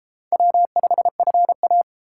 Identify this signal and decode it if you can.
Morse code, heard as a keyed tone, about 33 wpm, 700 Hz. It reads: W5FA